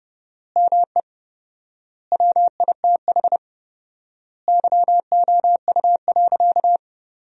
Morse code, heard as a keyed tone, about 30 wpm, 700 Hz. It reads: ME WITH YOU.
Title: 4x Word-Spacing - Sentences from Top 200 Words - 30wpm